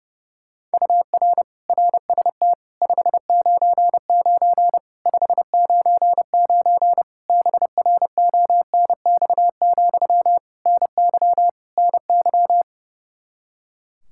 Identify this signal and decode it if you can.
Morse code, heard as a keyed tone, about 30 wpm, 700 Hz. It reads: UR RST 599 599 BRONX, NY NY
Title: Practice QSO - Exchange 1 - Beginning - 30wpm